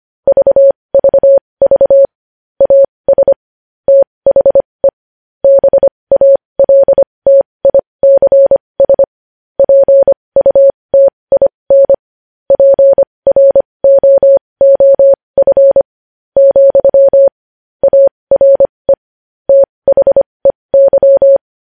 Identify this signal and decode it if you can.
Morse code, heard as a keyed tone, about 25 wpm, spaced as for 19 wpm, 550 Hz. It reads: VVV AS THE BALTICS PUTIN PROOF, ARE THEY
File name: podcast / News headlines-1x-25x20